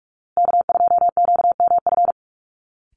Morse code, heard as a keyed tone, about 44 wpm, 700 Hz. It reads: K2QMF
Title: Call Signs Repeated 2x - 45wpm